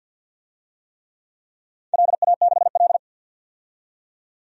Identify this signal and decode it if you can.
Morse code, heard as a keyed tone, about 50 wpm, 700 Hz. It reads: LA6L
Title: DXCC-100 Call Signs - 4 Characters - 50wpm